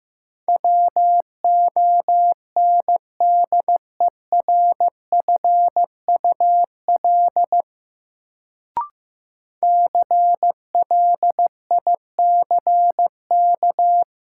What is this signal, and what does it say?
Morse code, heard as a keyed tone, about 15 wpm, 700 Hz. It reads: WONDERFUL CLICK
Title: Sentences from Top 1100 Words - New Word Drill Alt - 15wpm